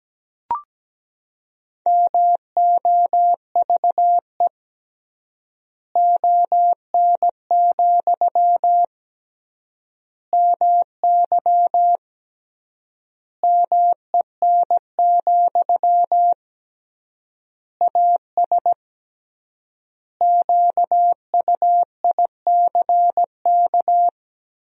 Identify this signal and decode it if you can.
Morse code, heard as a keyed tone, about 17 wpm, 700 Hz. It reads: MOVE ON, MY MEN, AS QUICK
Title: 3x Word-Spacing - Sentences from Top 500 Words - 17wpm